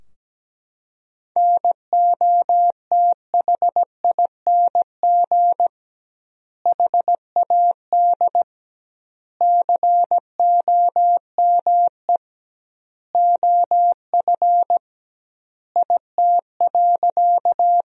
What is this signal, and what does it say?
Morse code, heard as a keyed tone, about 17 wpm, 700 Hz. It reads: NOTHING HAD COME OF IT.
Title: 2x Word-Spacing - Sentences from Top 500 Words - 17wpm